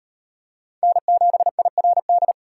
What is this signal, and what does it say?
Morse code, heard as a keyed tone, about 36 wpm, 700 Hz. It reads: N7IRD